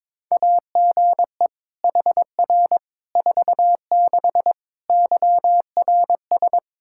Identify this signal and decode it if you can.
Morse code, heard as a keyed tone, about 22 wpm, 700 Hz. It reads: AGE HR 46 YRS
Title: Practice QSO - Exchange 3 - More Info - 22wpm